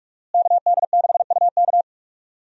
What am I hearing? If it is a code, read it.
Morse code, heard as a keyed tone, about 46 wpm, 700 Hz. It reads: KD6UK
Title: USA Call Signs - 5 Characters - Repeated 2x - 45wpm